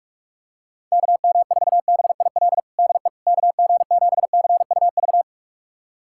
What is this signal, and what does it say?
Morse code, heard as a keyed tone, about 46 wpm, 700 Hz. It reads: KM4BIL DE KG7CUV
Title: USA Callsign DE Callsign - 6 Characters - 45wpm